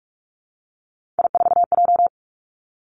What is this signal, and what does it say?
Morse code, heard as a keyed tone, about 46 wpm, 700 Hz. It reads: I4J
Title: Three Challenging Letter-Number Combinations 45wpm